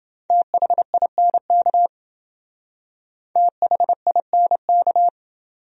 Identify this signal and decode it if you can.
Morse code, heard as a keyed tone, about 27 wpm, 700 Hz. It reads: THINK THINK